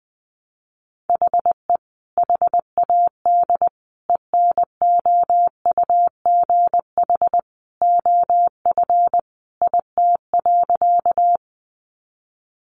Morse code, heard as a keyed tone, about 20 wpm, 700 Hz. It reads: HE HAD ENOUGH OF IT.
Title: Sentences from Top 400 Words 20wpm